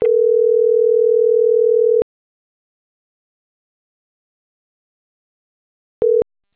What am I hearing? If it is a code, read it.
Ringback tone (North America), 440 + 480 Hz.